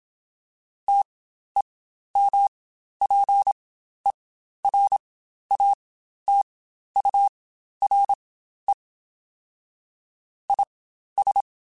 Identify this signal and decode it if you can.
Morse code, heard as a keyed tone, about 27 wpm, spaced as for 12 wpm, 800 Hz. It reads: TEMPERATURE IS
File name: Spaced Characters / Spaced Characters-mp3s / Weather4 24 sp @25